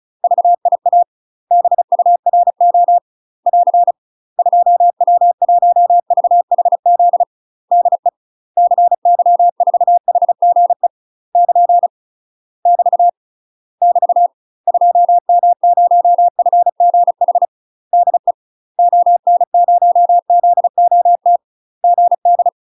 Morse code, heard as a keyed tone, about 34 wpm, 700 Hz. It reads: VIA BURO <AR> 2W1VHZ DE CY4HGE <KN> <BT> <BT> 2M0FGH DE ON0ZOT GD
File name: UK MASTER QSOs TRAINING / 35WPM / UK MASTER QSOs 0024